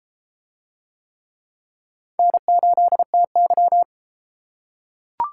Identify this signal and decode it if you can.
Morse code, heard as a keyed tone, about 32 wpm, 700 Hz. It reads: N8TY